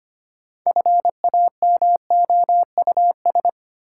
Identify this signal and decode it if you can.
Morse code, heard as a keyed tone, about 25 wpm, 700 Hz. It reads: FAMOUS
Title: Sentences from Top 1500 Words - New Word Drill - 25wpm